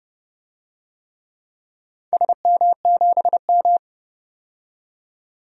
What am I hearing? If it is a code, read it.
Morse code, heard as a keyed tone, about 32 wpm, 700 Hz. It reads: SM7M
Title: DXCC-100 Call Signs - 4 Characters - 30wpm